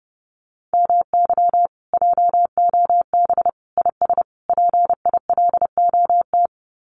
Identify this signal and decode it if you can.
Morse code, heard as a keyed tone, about 30 wpm, 700 Hz. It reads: MY JOB IS PILOT
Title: Common QSO Phrases Encore 30wpm